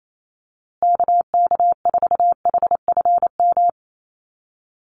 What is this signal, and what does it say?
Morse code, heard as a keyed tone, about 28 wpm, 700 Hz. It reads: KK4HFM
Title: USA Call Signs Repeated 3x 28wpm